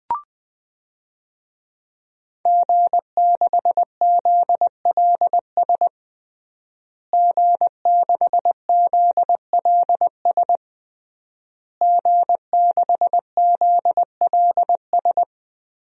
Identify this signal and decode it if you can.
Morse code, heard as a keyed tone, about 20 wpm, 700 Hz. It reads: G6ZLS G6ZLS G6ZLS
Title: DXCC-100 Call Signs - 5 Characters - Repeated 3x - 20wpm